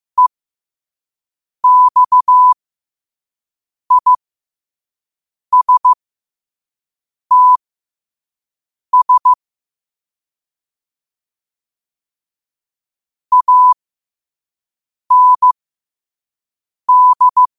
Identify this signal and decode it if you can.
Morse code, heard as a keyed tone, about 15 wpm, spaced as for 5 wpm, 1000 Hz. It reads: EXISTS AND
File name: qotdfiles / 5wpm